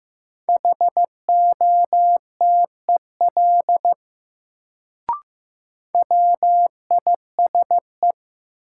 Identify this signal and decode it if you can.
Morse code, heard as a keyed tone, about 15 wpm, 700 Hz. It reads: HOTEL WISE
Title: Sentences from Top 1600 Words - New Word Drill - 15wpm